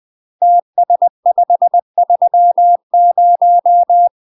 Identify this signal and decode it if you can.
Morse code, heard as a keyed tone, about 20 wpm, 700 Hz. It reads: TS530
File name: UK MASTER QSOs TRAINING/20WPM